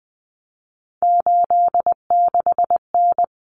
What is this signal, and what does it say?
Morse code, heard as a keyed tone, about 20 wpm, 700 Hz. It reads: 86N
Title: Common Cut Numbers - 3-Digit 20wpm